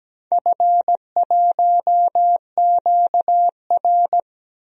Morse code, heard as a keyed tone, about 17 wpm, 700 Hz. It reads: F1QR
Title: DXCC-100 Call Signs - 4 Characters - Repeated 3x - 17wpm